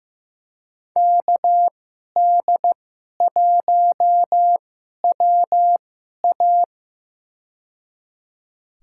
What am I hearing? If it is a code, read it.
Morse code, heard as a keyed tone, about 15 wpm, spaced as for 11 wpm, 700 Hz. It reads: KD1WA
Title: USA Call Signs - 5 Characters - 2x Character Spacing - 15wpm